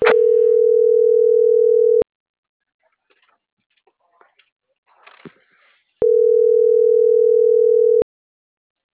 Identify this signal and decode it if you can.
Ringback tone (North America), 440 + 480 Hz.